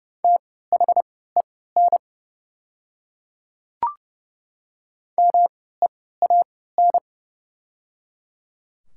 Morse code, heard as a keyed tone, about 30 wpm, spaced as for 17 wpm, 700 Hz. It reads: THEN MEAN
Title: 3x Character Spacing - Top 300 Words - 30wpm